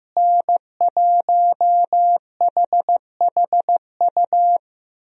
Morse code, heard as a keyed tone, about 15 wpm, 700 Hz. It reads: N1HHU
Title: USA Call Signs - 5 Characters - Repeated 2x - 15wpm